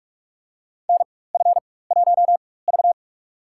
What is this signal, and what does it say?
Morse code, heard as a keyed tone, about 46 wpm, spaced as for 21 wpm, 690 Hz. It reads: NF1V
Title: USA Call Signs - 4 Characters - 4x Character Spacing - 45wpm